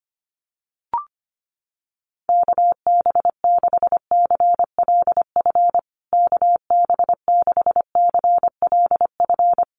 Morse code, heard as a keyed tone, about 25 wpm, 700 Hz. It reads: KB6CLF KB6CLF
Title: USA Call Signs Repeated 3x 25wpm